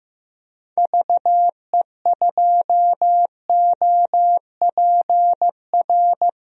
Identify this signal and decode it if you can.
Morse code, heard as a keyed tone, about 15 wpm, 700 Hz. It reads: VE2OPR
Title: DXCC-100 Call Signs - 6 Characters - 15wpm